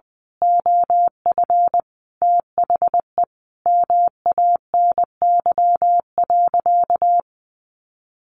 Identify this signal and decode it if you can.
Morse code, heard as a keyed tone, about 20 wpm, 700 Hz. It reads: OF THE MANY.
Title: Sentences from Top 1300 Words 20wpm